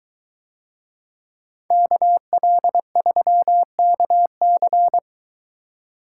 Morse code, heard as a keyed tone, about 22 wpm, 700 Hz. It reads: KL3KC